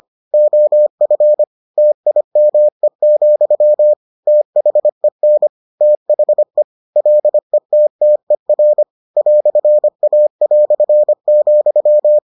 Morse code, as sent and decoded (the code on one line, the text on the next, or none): --- ..-. / - .. -- . --..-- / - .... -. / - .... / .-.. - - . .-. / .-..-. .- .-..-. --..--
OF TIME, THEN THE LETTER "A",